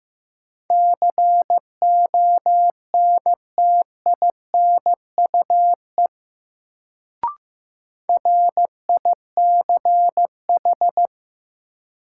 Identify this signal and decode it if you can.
Morse code, heard as a keyed tone, about 15 wpm, 700 Hz. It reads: CONTINUE RICH
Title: Sentences from Top 1100 Words - New Word Drill Alt - 15wpm